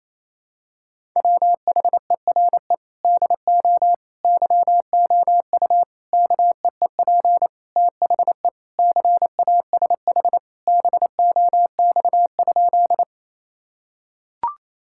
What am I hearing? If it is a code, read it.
Morse code, heard as a keyed tone, about 27 wpm, 700 Hz. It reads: WHERE DO YOU KEEP THE CASH BOX?
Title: Sentences from Top 1000 Words - Review Part 5 - 28wpm